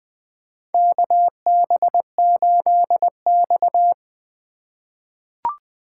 Morse code, heard as a keyed tone, about 21 wpm, 700 Hz. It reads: KB8X
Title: USA Call Signs - 4 Characters - 4x Character Spacing - 20wpm